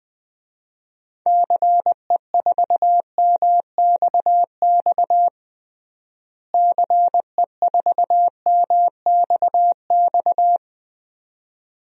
Morse code, heard as a keyed tone, about 20 wpm, 700 Hz. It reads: CE4MXX CE4MXX
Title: DXCC-100 Call Signs - 6 Characters - Repeated 2x - 20wpm